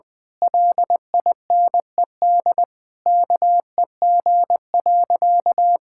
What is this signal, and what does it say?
Morse code, heard as a keyed tone, about 21 wpm, 700 Hz. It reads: LINED KEG.
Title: Pangram Challenge v2 - 20wpm